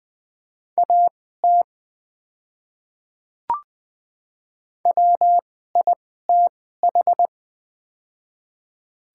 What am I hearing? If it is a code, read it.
Morse code, heard as a keyed tone, about 20 wpm, spaced as for 15 wpm, 700 Hz. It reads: AT WITH
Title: 2x Character Spacing - Top 200 Words - 20wpm